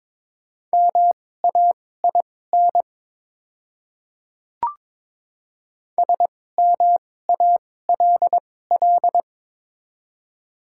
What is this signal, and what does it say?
Morse code, heard as a keyed tone, about 22 wpm, spaced as for 16 wpm, 700 Hz. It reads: MAIN SMALL
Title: 2x Character Spacing - Top 500 Words - 22wpm